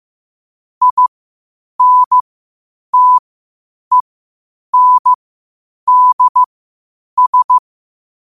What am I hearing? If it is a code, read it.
Morse code, heard as a keyed tone, about 15 wpm, spaced as for 8 wpm, 1000 Hz. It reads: INTENDS
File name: qotdfiles / 8wpm